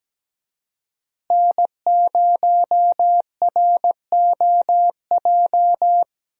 Morse code, heard as a keyed tone, about 17 wpm, 700 Hz. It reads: N0ROJ
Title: USA Call Signs - 5 Characters - Repeated 3x - 17wpm